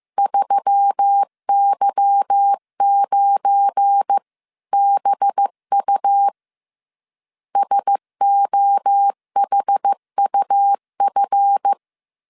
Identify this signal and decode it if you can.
Morse code, heard as a keyed tone, about 15 wpm, 800 Hz. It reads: 3Y9BU SOHUF
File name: CW / DET / 1 9 1500C S C 14wpm 3245